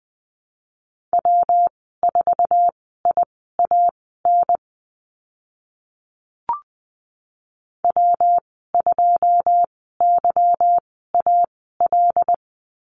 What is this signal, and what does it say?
Morse code, heard as a keyed tone, about 20 wpm, spaced as for 15 wpm, 700 Hz. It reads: W4IAN W2YAL
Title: 161 Lesson 26g - Letter g - Call Signs 20wpm